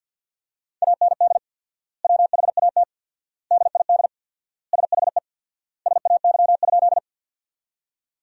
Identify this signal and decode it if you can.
Morse code, heard as a keyed tone, about 50 wpm, 700 Hz. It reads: AND WHAT DID SHE SAY?
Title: 4x Word-Spacing - Sentences from Top 200 Words - 50wpm